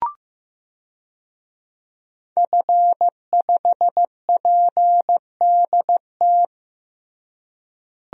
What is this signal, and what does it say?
Morse code, heard as a keyed tone, about 16 wpm, 700 Hz. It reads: F5PDT
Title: DXCC-100 Call Signs - 5 Characters - Repeated 2x - 15wpm